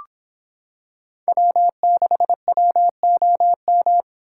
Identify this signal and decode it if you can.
Morse code, heard as a keyed tone, about 26 wpm, 700 Hz. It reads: W6WOM